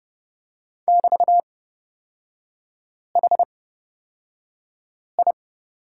Morse code, heard as a keyed tone, about 30 wpm, spaced as for 5 wpm, 700 Hz. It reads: <BT>HI
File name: Lesson 36e - BT - Three Characters 30wpm